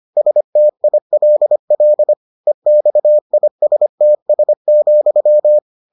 Morse code, heard as a keyed tone, about 25 wpm, 600 Hz. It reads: STILL EXISTS,